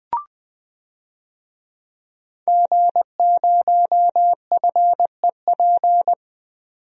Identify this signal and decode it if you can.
Morse code, heard as a keyed tone, about 20 wpm, 700 Hz. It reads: G0FEP
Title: DXCC-100 Call Signs - 5 Characters - 20wpm